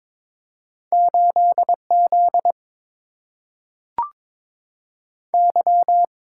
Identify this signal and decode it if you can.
Morse code, heard as a keyed tone, about 22 wpm, 700 Hz. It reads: 8 Z Y
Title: Single-Letter Single-Number 22wpm